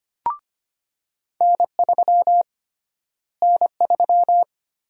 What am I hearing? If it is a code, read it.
Morse code, heard as a keyed tone, about 25 wpm, 700 Hz. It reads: N3 N3
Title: USA Call Sign Prefixes - Repeated 2x - 25wpm